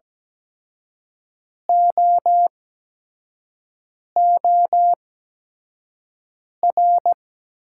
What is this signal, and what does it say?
Morse code, heard as a keyed tone, about 17 wpm, spaced as for 5 wpm, 700 Hz. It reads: OOR
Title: 8x Character Spacing - Top 300 Words - 17wpm